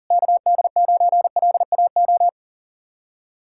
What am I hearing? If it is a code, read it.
Morse code, heard as a keyed tone, about 50 wpm, 700 Hz. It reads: KD9LAO
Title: Speed-Racing Common QSO Phrases - 40wpm